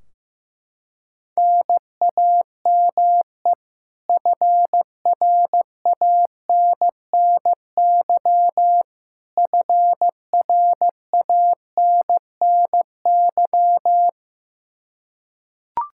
Morse code, heard as a keyed tone, about 15 wpm, 700 Hz. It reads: NAME FRANNY FRANNY
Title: Practice QSO - Exchange 1 - Beginning - 15wpm